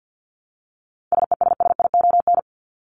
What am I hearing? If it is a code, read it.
Morse code, heard as a keyed tone, about 48 wpm, 700 Hz. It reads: SESSION
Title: Top 1501-1750 Words 50wpm